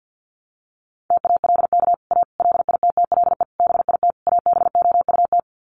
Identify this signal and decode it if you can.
Morse code, heard as a keyed tone, about 50 wpm, 700 Hz. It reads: TALK A LITTLE BIT ABOUT
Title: Sets of 5 Words Encore 50wpm